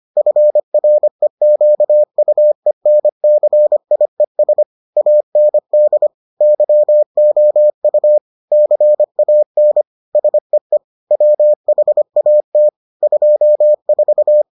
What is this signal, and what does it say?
Morse code, heard as a keyed tone, about 25 wpm, 600 Hz. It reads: FREQUENCIES AND YOU CAN SEE WHAT 24